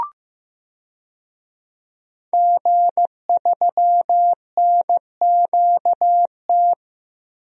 Morse code, heard as a keyed tone, about 15 wpm, 700 Hz. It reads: G3NQT
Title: DXCC-100 Call Signs - 5 Characters - Repeated 2x - 15wpm